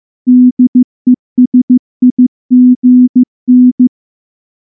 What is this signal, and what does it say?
Morse code, heard as a keyed tone, about 14 wpm, 250 Hz. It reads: DESIGN
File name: morse / morse-training-15WPM